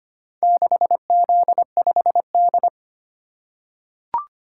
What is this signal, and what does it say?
Morse code, heard as a keyed tone, about 25 wpm, 700 Hz. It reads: 6Z5D
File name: practice / 25wpm / Four Letter-Number Combinations - 25wpm